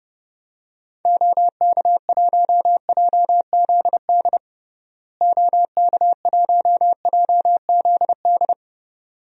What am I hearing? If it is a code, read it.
Morse code, heard as a keyed tone, about 31 wpm, 700 Hz. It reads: OK1JZD OK1JZD
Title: DXCC-100 Call Signs - 6 Characters - Repeated 2x - 30wpm